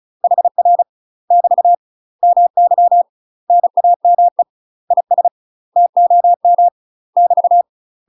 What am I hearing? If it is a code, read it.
Morse code, heard as a keyed tone, about 34 wpm, 700 Hz. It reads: HR <BT> MY NAME IS TOM <BT>
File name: UK MASTER QSOs TRAINING / 35WPM / UK MASTER QSOs 0003